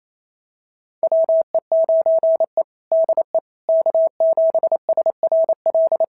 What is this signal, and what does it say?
Morse code, heard as a keyed tone, about 28 wpm, 650 Hz. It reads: WE9E DE K7SRL
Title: SOTA QSO Practice 28wpm